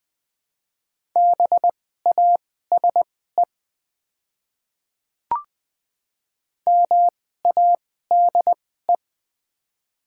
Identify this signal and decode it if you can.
Morse code, heard as a keyed tone, about 20 wpm, spaced as for 14 wpm, 690 Hz. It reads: BASE MADE
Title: 2x Character Spacing - Top 300 Words - 20wpm